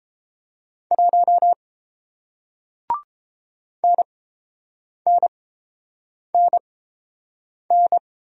Morse code, heard as a keyed tone, about 30 wpm, spaced as for 7 wpm, 700 Hz. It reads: 1 NNNN